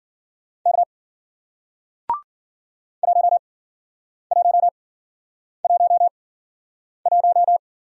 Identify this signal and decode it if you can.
Morse code, heard as a keyed tone, about 54 wpm, spaced as for 9 wpm, 700 Hz. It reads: K 1111